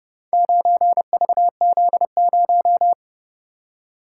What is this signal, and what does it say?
Morse code, heard as a keyed tone, about 29 wpm, 700 Hz. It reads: 9VZ0